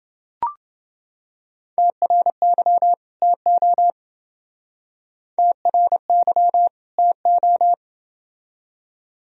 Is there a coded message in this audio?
Yes: Morse code from 1.8 to 7.7 s, heard as a keyed tone, about 31 wpm, 700 Hz. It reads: TRY TO TRY TO